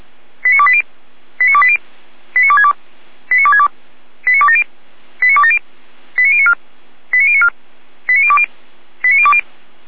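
Five-tone selective call (ZVEI), heard as quick runs of five tones, tones of about 70 ms. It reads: 89280 89280 89262 89262 89270 89270 89094 89094 89020 89020
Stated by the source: ZVEI five-tone selective call received by SDR.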